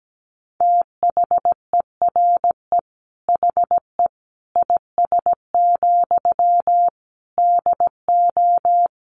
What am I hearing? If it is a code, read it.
Morse code, heard as a keyed tone, about 17 wpm, 700 Hz. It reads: THERE HE IS, DO
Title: Sentences from Top 100 Words 17wpm